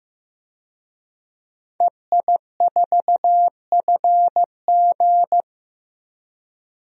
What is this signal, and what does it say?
Morse code, heard as a keyed tone, about 15 wpm, 700 Hz. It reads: EI4FG
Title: DXCC-100 Call Signs - 5 Characters - 15wpm